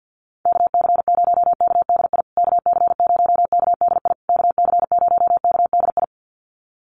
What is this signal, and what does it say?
Morse code, heard as a keyed tone, about 50 wpm, 700 Hz. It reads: KC0KDI KC0KDI KC0KDI
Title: USA Call Signs Repeated 3x 50wpm